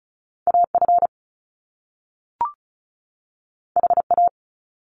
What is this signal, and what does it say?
Morse code, heard as a keyed tone, about 36 wpm, 700 Hz. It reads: AF HA
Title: Top Two Letter Combinations in Words 35wpm